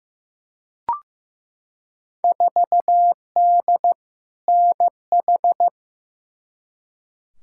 Morse code, heard as a keyed tone, about 15 wpm, 700 Hz. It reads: 4D NH
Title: Field Day Exchanges Encore - 15wpm